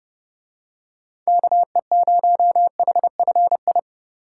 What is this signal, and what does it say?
Morse code, heard as a keyed tone, about 28 wpm, 700 Hz. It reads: KE0HFI